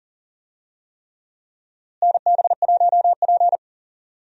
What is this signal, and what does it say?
Morse code, heard as a keyed tone, about 40 wpm, 700 Hz. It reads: NB1P
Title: USA Call Signs - 4 Characters - Repeated 3x - 40wpm